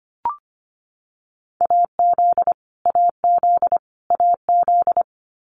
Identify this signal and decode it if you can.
Morse code, heard as a keyed tone, about 25 wpm, 700 Hz. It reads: AZ AZ AZ
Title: US States Repeated 3x 25wpm